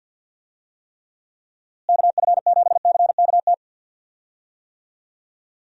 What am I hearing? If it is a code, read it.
Morse code, heard as a keyed tone, about 51 wpm, 700 Hz. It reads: KF7CKT
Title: USA Call Signs - 6 Characters - Repeated 2x - 50wpm